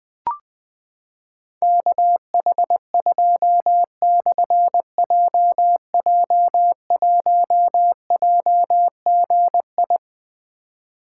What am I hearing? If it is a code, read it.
Morse code, heard as a keyed tone, about 20 wpm, 700 Hz. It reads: KH2/JJ1JGI
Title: Difficult Call Signs 20wpm